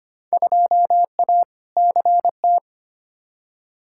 Morse code, heard as a keyed tone, about 25 wpm, 700 Hz. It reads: 2A CT
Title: Field Day Exchanges Encore - 25wpm